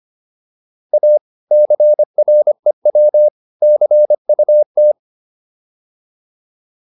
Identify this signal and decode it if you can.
Morse code, heard as a keyed tone, about 25 wpm, 600 Hz. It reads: A CREW CUT
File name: PR302 25